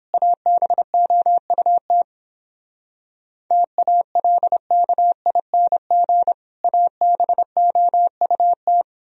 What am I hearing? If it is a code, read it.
Morse code, heard as a keyed tone, about 31 wpm, 700 Hz. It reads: ABOUT TALKING ABOUT